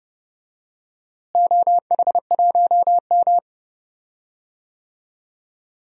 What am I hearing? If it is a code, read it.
Morse code, heard as a keyed tone, about 30 wpm, 700 Hz. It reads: OH1M
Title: DXCC-100 Call Signs - 4 Characters - Repeated 3x - 30wpm